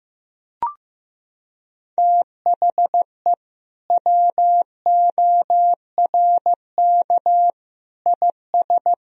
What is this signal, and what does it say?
Morse code, heard as a keyed tone, about 15 wpm, 700 Hz. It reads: THE WORK IS
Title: Sentences from Top 1000 Words - Review Part 4 - 15wpm